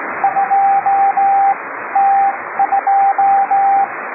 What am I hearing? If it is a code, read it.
Morse code, heard as a keyed tone, about 18 wpm, spaced as for 14 wpm, 790 Hz, with radio noise. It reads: 2T2